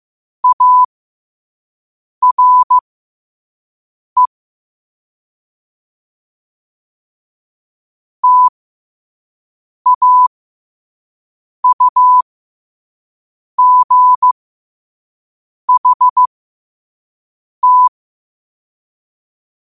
Morse code, heard as a keyed tone, about 15 wpm, spaced as for 5 wpm, 1000 Hz. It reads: ARE TAUGHT
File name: qotdfiles / 5wpm